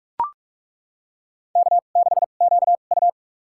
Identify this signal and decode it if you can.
Morse code, heard as a keyed tone, about 41 wpm, spaced as for 31 wpm, 700 Hz. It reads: K6QU
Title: USA Call Signs - 4 Characters - 2x Character Spacing - 45wpm